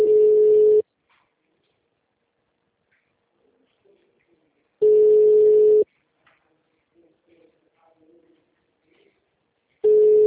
Ringback tone (Europe, 425 Hz).